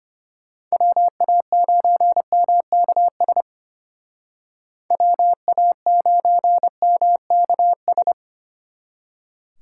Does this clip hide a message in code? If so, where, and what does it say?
Yes: Morse code from 0.7 to 8.1 s, heard as a keyed tone, about 33 wpm, 700 Hz. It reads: WA9MKS WA9MKS